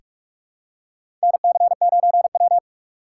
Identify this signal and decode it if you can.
Morse code, heard as a keyed tone, about 46 wpm, 700 Hz. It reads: NC9W
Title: K1USN SST QSO Practice - 45wpm